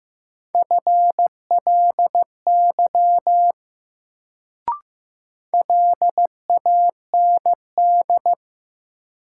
Morse code, heard as a keyed tone, about 15 wpm, 700 Hz. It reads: FLY LAND
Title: Sentences from Top 500 Words - New Word Drill - 15wpm